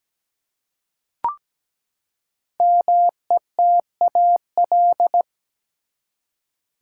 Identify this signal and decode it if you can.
Morse code, heard as a keyed tone, about 17 wpm, 700 Hz. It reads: METAL
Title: Five Letter Words - 17wpm